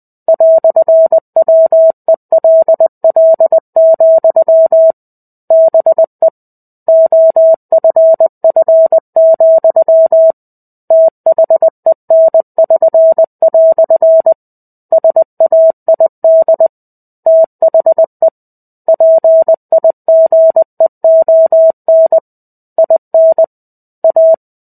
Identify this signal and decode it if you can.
Morse code, heard as a keyed tone, about 20 wpm, 650 Hz. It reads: "WELL, BE OFF, THEN<SN>" SAID THE PIGEON IN A
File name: 05-Chapter-05 20 WPM